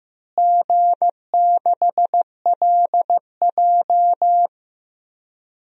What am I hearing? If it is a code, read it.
Morse code, heard as a keyed tone, about 16 wpm, 700 Hz. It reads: G6LJ
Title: DXCC-100 Call Signs - 4 Characters - Repeated 3x - 15wpm